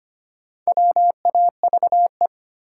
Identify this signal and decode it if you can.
Morse code, heard as a keyed tone, about 25 wpm, 700 Hz. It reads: WAVE
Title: Sentences from Top 1400 Words - New Word Drill Alt - 25wpm